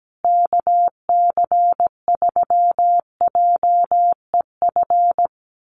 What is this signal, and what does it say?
Morse code, heard as a keyed tone, about 17 wpm, 700 Hz. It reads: KC3JEF
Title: USA Call Signs Repeated 3x 17wpm